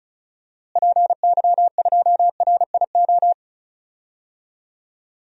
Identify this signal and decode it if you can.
Morse code, heard as a keyed tone, about 34 wpm, 700 Hz. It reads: PY2RIO
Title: DXCC-100 Call Signs - 6 Characters - Repeated 3x - 35wpm